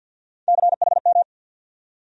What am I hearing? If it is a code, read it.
Morse code, heard as a keyed tone, about 47 wpm, 700 Hz. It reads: CHM
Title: USA Call Sign Postfixes - 3 Letters 50wpm